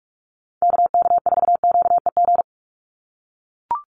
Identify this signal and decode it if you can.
Morse code, heard as a keyed tone, about 44 wpm, 700 Hz. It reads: KK4QEG
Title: USA Call Signs 45wpm